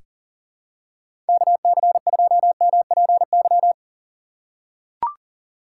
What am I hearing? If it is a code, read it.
Morse code, heard as a keyed tone, about 40 wpm, 700 Hz. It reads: KC2MPY
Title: USA Call Signs - 6 Characters - 6x Character Spacing - 40wpm